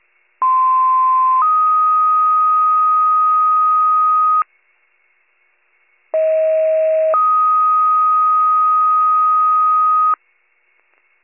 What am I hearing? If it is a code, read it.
Two-tone page, heard as a one-second tone, then a three-second tone, about 1040 then 1250 Hz, about 640 then 1250 Hz. It is heard from 0.4 to 10.1 s.